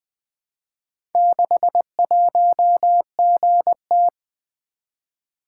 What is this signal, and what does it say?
Morse code, heard as a keyed tone, about 20 wpm, 700 Hz. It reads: 61GT
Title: Four Letter-Number Combinations - 20wpm